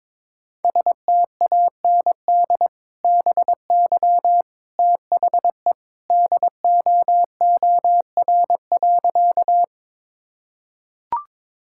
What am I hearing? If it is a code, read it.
Morse code, heard as a keyed tone, about 22 wpm, 700 Hz. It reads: STAND BY THE DOOR.
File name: practice / 22wpm / Sentences with 2-Word Phrases - 22wpm